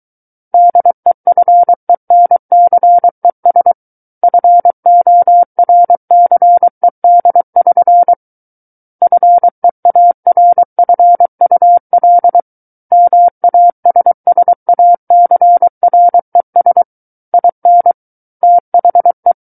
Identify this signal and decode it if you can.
Morse code, heard as a keyed tone, about 23 wpm, 700 Hz. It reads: DEFENCES FORCED<SN> FEARFUL MASSACRES IN THE
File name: Book One / 14-Chapter 14 - 23 WPM